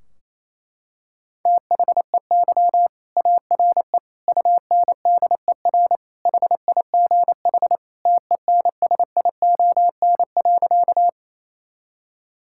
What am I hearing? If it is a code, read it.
Morse code, heard as a keyed tone, about 28 wpm, 700 Hz. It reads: THEY ARE UNDER HIGH TENSION.
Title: Sentences from Top 1600 Words 28wpm